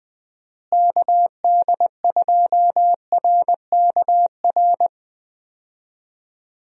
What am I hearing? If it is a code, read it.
Morse code, heard as a keyed tone, about 20 wpm, 700 Hz. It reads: KD2RKR
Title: USA Call Signs - 6 Characters - Repeated 2x - 20wpm